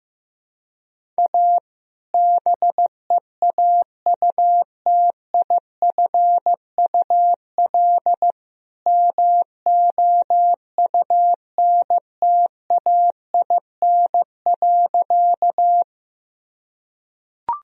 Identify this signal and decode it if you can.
Morse code, heard as a keyed tone, about 15 wpm, 700 Hz. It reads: A BEAUTIFUL MOUNTAIN.
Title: Sentences from Top 1000 Words - Review Part 1 - 15wpm